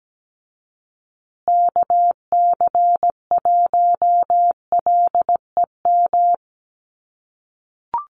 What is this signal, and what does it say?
Morse code, heard as a keyed tone, about 17 wpm, 700 Hz. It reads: KC1LEM
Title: USA Call Signs Repeated 3x 17wpm